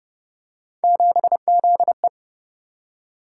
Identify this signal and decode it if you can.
Morse code, heard as a keyed tone, about 30 wpm, 700 Hz. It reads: 7ZE